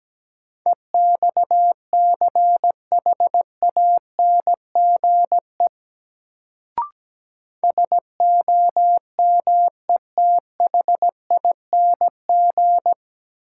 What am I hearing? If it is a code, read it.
Morse code, heard as a keyed tone, about 17 wpm, 700 Hz. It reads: EXCHANGE SOMETHING
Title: Sets of 5 Words - New Word Drill - 17wpm